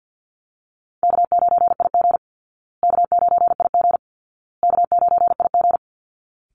Morse code, heard as a keyed tone, about 50 wpm, 700 Hz. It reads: K9IG K9IG K9IG
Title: Call Signs Repeated 2x - 50wpm